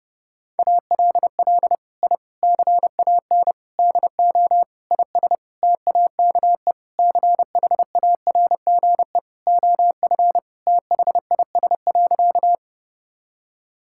Morse code, heard as a keyed tone, about 32 wpm, 700 Hz. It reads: ALL I CAN DO IS TAKE CHARGE OF THIS.
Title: Sentences from Top 1000 Words - Review Part 6 - 30wpm